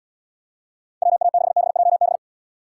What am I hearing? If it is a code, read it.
Morse code, heard as a keyed tone, about 69 wpm, 700 Hz. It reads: WA6BJB